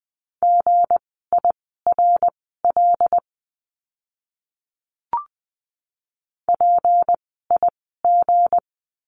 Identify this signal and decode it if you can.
Morse code, heard as a keyed tone, about 20 wpm, spaced as for 14 wpm, 700 Hz. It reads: GIRL PIG
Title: 160 Lesson 26f - Letter g - Words 20wpm